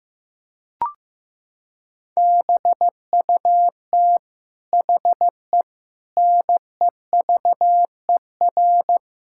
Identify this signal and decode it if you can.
Morse code, heard as a keyed tone, about 15 wpm, 700 Hz. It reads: BUT HE NEVER
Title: Sentences from Top 1600 Words 15wpm